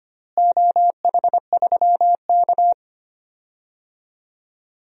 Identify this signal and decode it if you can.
Morse code, heard as a keyed tone, about 25 wpm, 700 Hz. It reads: OH3K
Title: DXCC-100 Call Signs - 4 Characters - Repeated 3x - 25wpm